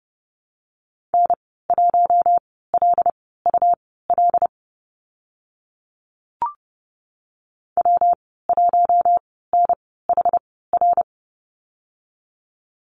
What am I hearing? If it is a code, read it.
Morse code, heard as a keyed tone, about 30 wpm, spaced as for 17 wpm, 700 Hz. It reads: N1LUL W1NHR
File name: Lesson 19g - Letter w - Call Signs 30wpm